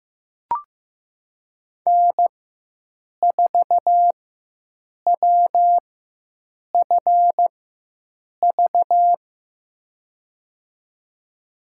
Morse code, heard as a keyed tone, about 15 wpm, spaced as for 7 wpm, 700 Hz. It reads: N4WFV
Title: USA Call Signs - 5 Characters - 4x Character Spacing - 15wpm